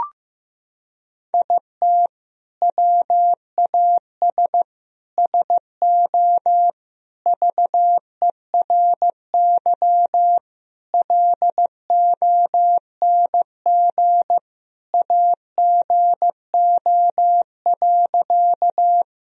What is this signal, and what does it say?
Morse code, heard as a keyed tone, about 15 wpm, 700 Hz. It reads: IT WAS SO VERY LONG AGO.
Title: Sentences from Top 1000 Words - Review Part 3 - 15wpm